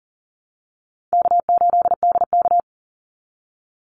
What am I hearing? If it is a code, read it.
Morse code, heard as a keyed tone, about 40 wpm, 700 Hz. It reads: K8DK
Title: Call Signs Repeated 2x - 40wpm